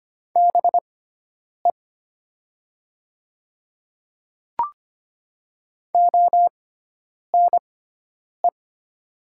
Morse code, heard as a keyed tone, about 25 wpm, spaced as for 9 wpm, 700 Hz. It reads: BE ONE
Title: 6x Character Spacing - Top 100 Words - 25wpm